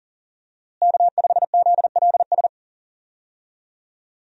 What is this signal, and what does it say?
Morse code, heard as a keyed tone, about 37 wpm, 700 Hz. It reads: K5ZLS
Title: USA Call Signs - 5 Characters - Repeated 3x - 40wpm